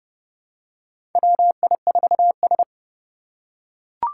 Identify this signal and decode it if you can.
Morse code, heard as a keyed tone, about 29 wpm, 700 Hz. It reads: WI4S
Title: USA Call Signs - 4 Characters - 3x Character Spacing - 30wpm